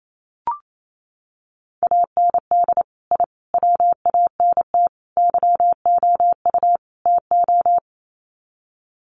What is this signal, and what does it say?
Morse code, heard as a keyed tone, about 28 wpm, 700 Hz. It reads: AND I WANT YOU TO
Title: Sets of 5 Words Encore 28wpm